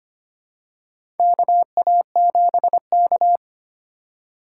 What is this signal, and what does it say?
Morse code, heard as a keyed tone, about 25 wpm, 700 Hz. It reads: KA7K
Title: USA Call Signs - 4 Characters - 4x Character Spacing - 25wpm